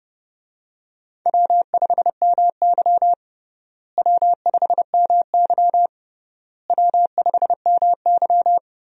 Morse code, heard as a keyed tone, about 30 wpm, 700 Hz. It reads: W5MY W5MY W5MY
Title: USA Call Signs - 4 Characters - Repeated 3x - 30wpm